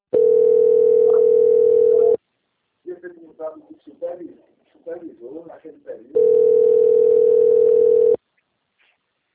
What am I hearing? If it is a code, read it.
Ringback tone (North America), 440 + 480 Hz, with speech.
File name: out- - - - - .3444